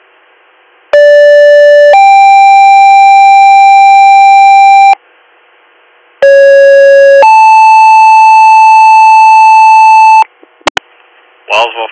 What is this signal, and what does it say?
Two-tone page, heard as a one-second tone, then a three-second tone, about 580 then 790 Hz, about 550 then 880 Hz, with speech. It is heard from 0.9 to 10.2 s.